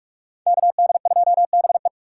Morse code, heard as a keyed tone, about 44 wpm, 700 Hz. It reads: KD2BE
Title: USA Call Signs - 5 Characters - 6x Character Spacing - 45wpm